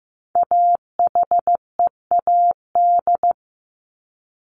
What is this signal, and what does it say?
Morse code, heard as a keyed tone, about 15 wpm, 700 Hz. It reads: AHEAD
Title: Top 1001-1250 Words 15wpm